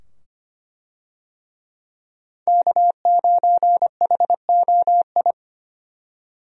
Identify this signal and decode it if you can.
Morse code, heard as a keyed tone, about 25 wpm, 690 Hz. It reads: K9HOI
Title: USA Call Signs - 5 Characters - 25wpm